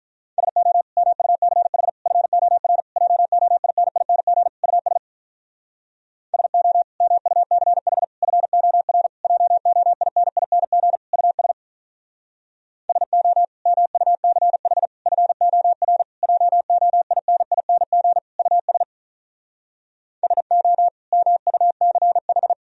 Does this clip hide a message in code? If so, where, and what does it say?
Yes: Morse code from 0.4 to 22.5 s, heard as a keyed tone, about 46 wpm, 700 Hz. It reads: SO MUCH FOR JOINING US SO MUCH FOR JOINING US SO MUCH FOR JOINING US SO MUCH